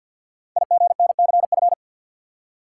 Morse code, heard as a keyed tone, about 50 wpm, 700 Hz. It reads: IGNCF
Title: Code Groups - 5 Letters - 50wpm